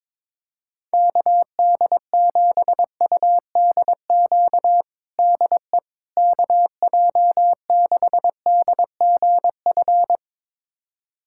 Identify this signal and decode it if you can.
Morse code, heard as a keyed tone, about 22 wpm, 700 Hz. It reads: KD7UDQ DE KJ6DGF
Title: USA Callsign DE Callsign - 6 Characters - 22wpm